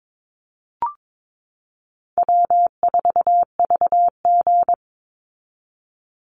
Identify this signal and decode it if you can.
Morse code, heard as a keyed tone, about 22 wpm, 700 Hz. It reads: W4VG
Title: Easy Call Signs Repeated 3x - 22wpm